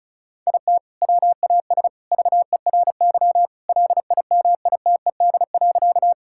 Morse code, heard as a keyed tone, about 36 wpm, 700 Hz. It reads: IT WAS VERY LIMITED.
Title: Sentences from Top 1000 Words - Review Part 6 - 35wpm